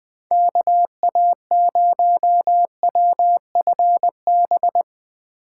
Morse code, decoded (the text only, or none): KA0WFB